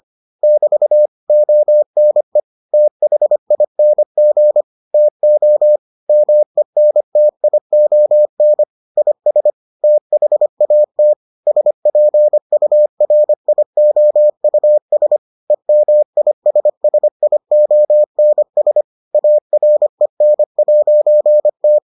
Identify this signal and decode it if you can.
Morse code, heard as a keyed tone, about 25 wpm, 600 Hz. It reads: <BT> ONE THING TO MENTION IS THAT SPURIOUS EMISSIONS AREN'T